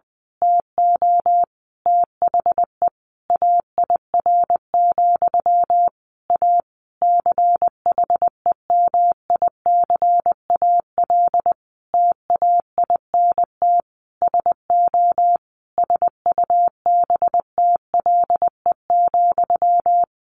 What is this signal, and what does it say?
Morse code, heard as a keyed tone, about 20 wpm, 700 Hz. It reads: TO THE AIR, A CHEMICAL TAINT SO SUBTLE,
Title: A Sound Of Thunder 20wpm